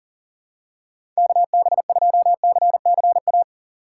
Morse code, heard as a keyed tone, about 40 wpm, 700 Hz. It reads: KB2CCA
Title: USA Call Signs - 6 Characters - 8x Character Spacing - 40wpm